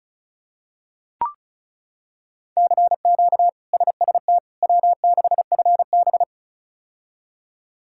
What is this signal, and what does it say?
Morse code, heard as a keyed tone, about 34 wpm, 700 Hz. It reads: CQ SST W6FB
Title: K1USN SST QSO Practice - 35wpm